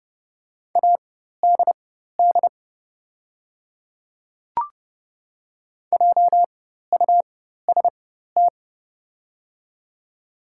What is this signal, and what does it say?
Morse code, heard as a keyed tone, about 32 wpm, spaced as for 14 wpm, 700 Hz. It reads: ADD JUST
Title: 4x Character Spacing - Top 200 Words - 30wpm